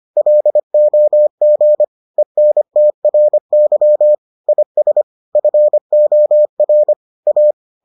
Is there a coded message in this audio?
Yes: Morse code from 0.2 to 7.5 s, heard as a keyed tone, about 25 wpm, 600 Hz. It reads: LOG ENTRY IS FOR A